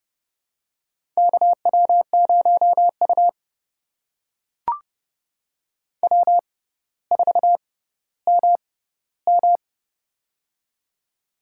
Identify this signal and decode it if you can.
Morse code, heard as a keyed tone, about 30 wpm, spaced as for 15 wpm, 700 Hz. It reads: KW0U W4MM
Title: USA Call Signs - 4 Characters - 6x Character Spacing - 30wpm